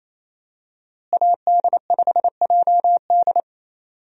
Morse code, heard as a keyed tone, about 28 wpm, 700 Hz. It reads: AD5JD
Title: USA Call Signs - 5 Characters - Repeated 2x - 28wpm